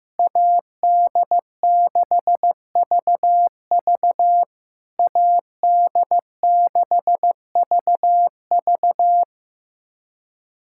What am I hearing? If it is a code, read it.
Morse code, heard as a keyed tone, about 15 wpm, 700 Hz. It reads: AD6VV AD6VV
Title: SOTA QSO Practice 15wpm